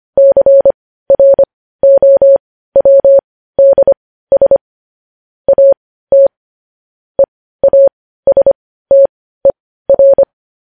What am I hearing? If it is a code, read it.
Morse code, heard as a keyed tone, about 25 wpm, spaced as for 15 wpm, 550 Hz. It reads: CROWDS AT EASTER
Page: News headlines in morse code at 15 wpm